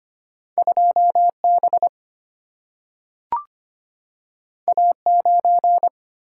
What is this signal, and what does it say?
Morse code, heard as a keyed tone, about 25 wpm, 700 Hz. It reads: 2 B A 9
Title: Single-Letter Single-Number 25wpm